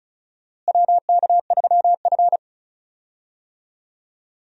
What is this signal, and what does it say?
Morse code, heard as a keyed tone, about 36 wpm, 700 Hz. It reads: WK3F
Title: USA Call Signs - 4 Characters - Repeated 2x - 35wpm